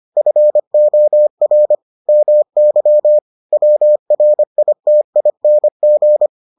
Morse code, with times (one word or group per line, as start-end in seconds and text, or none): 0.2-1.8 FOR
2.1-3.2 MY
3.5-6.3 WRITING